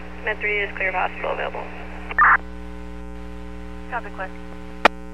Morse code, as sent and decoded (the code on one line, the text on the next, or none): none